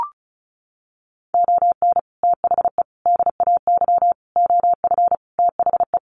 Morse code, heard as a keyed tone, about 34 wpm, 700 Hz. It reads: ON THE DAY OF THE
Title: Sets of 5 Words Encore 35wpm